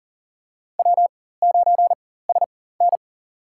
Morse code, heard as a keyed tone, about 40 wpm, spaced as for 19 wpm, 700 Hz. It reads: W9SN
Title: USA Call Signs - 4 Characters - 4x Character Spacing - 40wpm